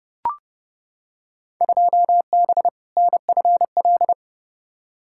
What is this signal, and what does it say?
Morse code, heard as a keyed tone, about 30 wpm, 700 Hz. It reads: 2B NFL
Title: Field Day Exchanges Encore - 30wpm